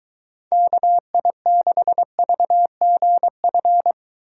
Morse code, heard as a keyed tone, about 23 wpm, 700 Hz. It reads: KI6VGF